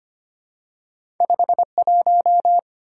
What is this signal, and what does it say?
Morse code, heard as a keyed tone, about 25 wpm, 700 Hz. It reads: 51
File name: Two Letter-Number Combinations - 25wpm